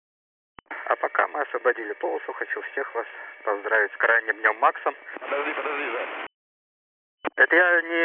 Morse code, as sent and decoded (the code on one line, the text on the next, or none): none